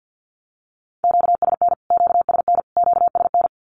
Morse code, heard as a keyed tone, about 50 wpm, 700 Hz. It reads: QSN QSN QSN
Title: Q-Codes Repeated 3x 50wpm